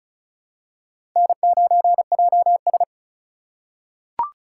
Morse code, heard as a keyed tone, about 36 wpm, 700 Hz. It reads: N9JS
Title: USA Call Signs - 4 Characters - 3x Character Spacing - 35wpm